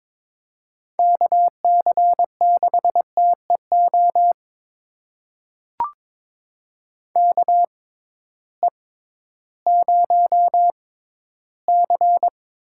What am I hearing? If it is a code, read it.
Morse code, heard as a keyed tone, about 21 wpm, spaced as for 13 wpm, 700 Hz. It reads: KC6TEO KE0C
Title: USA Call Signs - 6 Characters - 6x Character Spacing - 22wpm